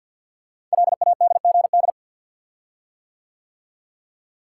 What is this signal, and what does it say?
Morse code, heard as a keyed tone, about 48 wpm, 700 Hz. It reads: LADGD
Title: Code Groups - 5 Letters - 50wpm